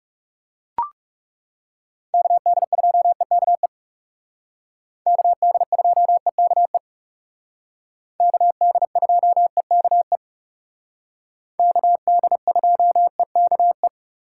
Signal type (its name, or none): Morse code